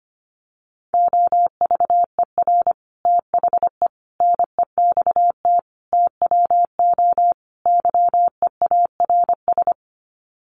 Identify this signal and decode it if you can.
Morse code, heard as a keyed tone, about 25 wpm, 700 Hz. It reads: OVER THE NEXT TWO YEARS
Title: Sets of 5 Words Encore2 25wpm